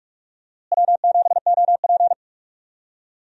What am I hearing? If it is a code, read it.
Morse code, heard as a keyed tone, about 46 wpm, 700 Hz. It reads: W7OP
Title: USA Call Signs - 4 Characters - Repeated 2x - 45wpm